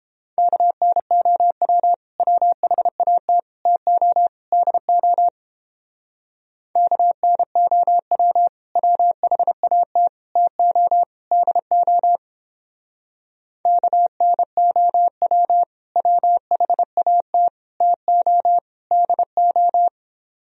Morse code, heard as a keyed tone, about 29 wpm, 700 Hz. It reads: KNOW WHAT TO DO KNOW WHAT TO DO KNOW WHAT TO DO